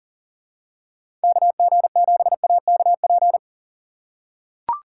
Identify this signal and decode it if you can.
Morse code, heard as a keyed tone, about 40 wpm, 700 Hz. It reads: KG7AKP
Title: USA Call Signs - 4x Character Spacing - 40wpm